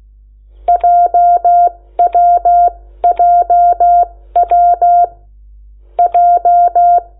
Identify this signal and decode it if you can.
Morse code, heard as a keyed tone, about 16 wpm, spaced as for 11 wpm, 700 Hz. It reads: JWJWJ
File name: Lecciones / CLASE Nº8